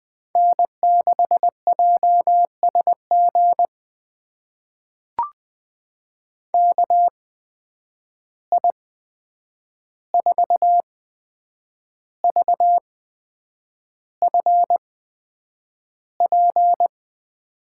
Morse code, heard as a keyed tone, about 20 wpm, spaced as for 8 wpm, 700 Hz. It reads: N6JSG KI4VFP
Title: USA Call Signs - 8x Character Spacing - 20wpm